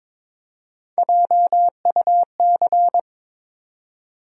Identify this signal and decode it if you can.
Morse code, heard as a keyed tone, about 22 wpm, 700 Hz. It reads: JUC
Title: USA Call Sign Postfixes - 3 Letters 22wpm